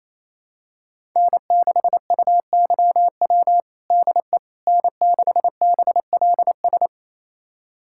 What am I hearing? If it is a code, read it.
Morse code, heard as a keyed tone, about 28 wpm, 700 Hz. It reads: N6UYW DE N6BLS
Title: USA Callsign DE Callsign - 5 Characters - 28wpm